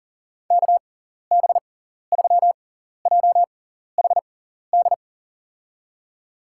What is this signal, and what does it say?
Morse code, heard as a keyed tone, about 37 wpm, spaced as for 14 wpm, 700 Hz. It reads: KB3JHD
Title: USA Call Signs - 6 Characters - 6x Character Spacing - 40wpm